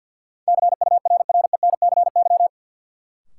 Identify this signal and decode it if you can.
Morse code, heard as a keyed tone, about 51 wpm, 700 Hz. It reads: CURRENCY
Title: Sentences from Top 1500 Words - New Word Drill - 50wpm